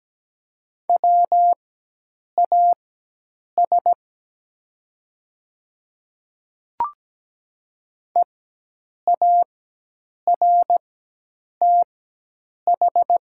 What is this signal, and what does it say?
Morse code, heard as a keyed tone, about 17 wpm, spaced as for 8 wpm, 700 Hz. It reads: WAS EARTH
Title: 4x Character Spacing - Top 300 Words - 17wpm